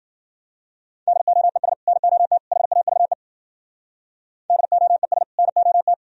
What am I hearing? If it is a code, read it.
Morse code, heard as a keyed tone, about 52 wpm, 700 Hz. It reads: DOES NOT HAVE DOES NOT